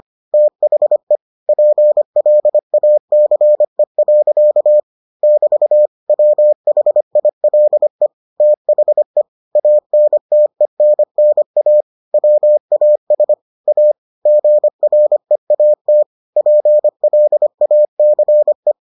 Morse code, heard as a keyed tone, about 25 wpm, 600 Hz. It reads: THE PLACE. <BT> WHILE THE ANTENNA WAS A GREAT PLACE